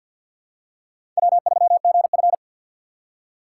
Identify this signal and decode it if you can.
Morse code, heard as a keyed tone, about 53 wpm, 700 Hz. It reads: W3GF
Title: USA Call Signs - 3x Character Spacing - 50wpm